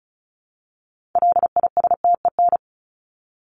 Morse code, heard as a keyed tone, about 34 wpm, 700 Hz. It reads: LISTEN
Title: Top 500 Words 35wpm